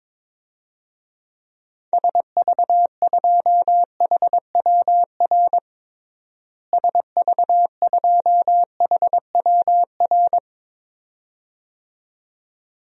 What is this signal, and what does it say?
Morse code, heard as a keyed tone, about 22 wpm, 700 Hz. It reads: SV2HWR SV2HWR
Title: DXCC-100 Call Signs - 6 Characters - Repeated 2x - 22wpm